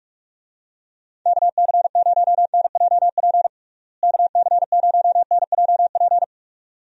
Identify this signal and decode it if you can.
Morse code, heard as a keyed tone, about 44 wpm, 700 Hz. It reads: KC0NJP KC0NJP
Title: USA Call Signs - 6 Characters - Repeated 2x - 45wpm